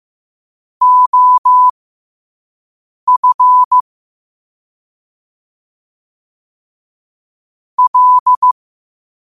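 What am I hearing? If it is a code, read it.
Morse code, heard as a keyed tone, about 15 wpm, spaced as for 5 wpm, 1000 Hz. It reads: OF L
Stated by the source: Morse code practice at 5 wpm.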